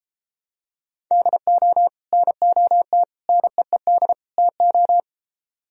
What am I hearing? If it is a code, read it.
Morse code, heard as a keyed tone, about 33 wpm, 700 Hz. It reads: DO NOT NEED TO